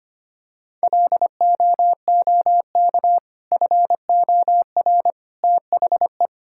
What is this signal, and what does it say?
Morse code, heard as a keyed tone, about 25 wpm, 700 Hz. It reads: LOOK FOR THE
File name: practice / 25wpm / Sentences with 2-Word Phrases - 25wpm